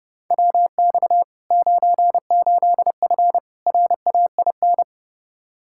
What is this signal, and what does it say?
Morse code, heard as a keyed tone, about 32 wpm, 700 Hz. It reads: WX 98F RAIN
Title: Practice QSO - Exchange 2 - Info - 30wpm